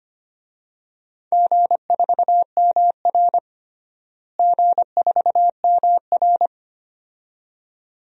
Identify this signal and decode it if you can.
Morse code, heard as a keyed tone, about 25 wpm, 700 Hz. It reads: G4MR G4MR
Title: DXCC-100 Call Signs - 4 Characters - Repeated 2x - 25wpm